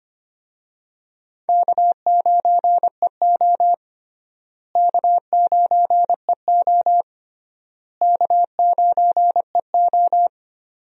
Morse code, heard as a keyed tone, about 25 wpm, 700 Hz. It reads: K9EO K9EO K9EO
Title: USA Call Signs - 4 Characters - Repeated 3x - 25wpm